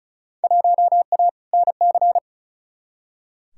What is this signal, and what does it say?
Morse code, heard as a keyed tone, about 36 wpm, 700 Hz. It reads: 1A NC
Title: Field Day Exchanges Encore - 35wpm